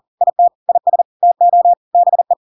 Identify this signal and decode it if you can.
Morse code, heard as a keyed tone, about 40 wpm, 700 Hz. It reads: IT IS TO BE
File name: Pinocchio / Pinocchio ch5 40 40 ik1hge